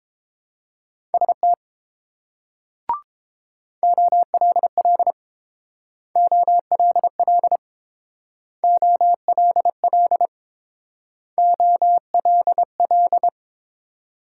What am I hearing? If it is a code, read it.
Morse code, heard as a keyed tone, about 29 wpm, 700 Hz. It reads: ST OLL OLL OLL OLL